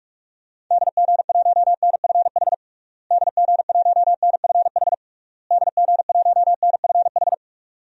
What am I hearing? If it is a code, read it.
Morse code, heard as a keyed tone, about 41 wpm, 700 Hz. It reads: DG1NFH DG1NFH DG1NFH
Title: DXCC-100 Call Signs - 6 Characters - Repeated 3x - 45wpm